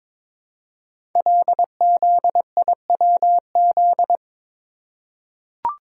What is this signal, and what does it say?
Morse code, heard as a keyed tone, about 22 wpm, 700 Hz. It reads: LZIWZ
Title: Code Groups - 5 Letters - 22wpm